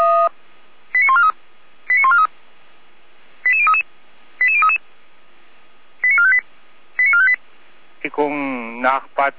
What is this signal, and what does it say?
Five-tone selective call (ZVEI), heard as quick runs of five tones, tones of about 70 ms, with speech. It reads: 89143 89143 88033 88033 89458 89458